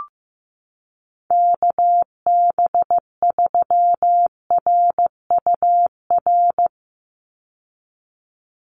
Morse code, heard as a keyed tone, about 15 wpm, 700 Hz. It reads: KB3RUR
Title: USA Call Signs 15wpm